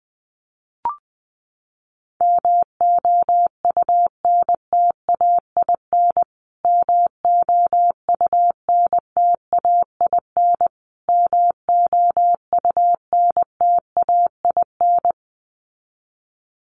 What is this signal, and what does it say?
Morse code, heard as a keyed tone, about 20 wpm, 700 Hz. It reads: MOUNTAIN MOUNTAIN MOUNTAIN
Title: Top 300 Words Repeated 20wpm